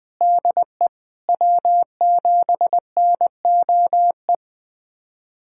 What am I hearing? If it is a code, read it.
Morse code, heard as a keyed tone, about 20 wpm, 700 Hz. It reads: DE W7NOE
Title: USA Callsign DE Callsign - 5 Characters - 20wpm